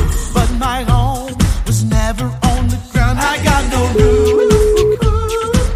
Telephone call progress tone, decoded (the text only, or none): ringback tone